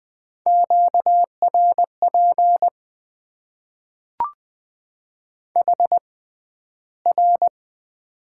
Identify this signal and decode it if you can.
Morse code, heard as a keyed tone, about 20 wpm, spaced as for 12 wpm, 700 Hz. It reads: QRP HR
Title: Top 90 QSO Elements - 6x Character Spacing - 20wpm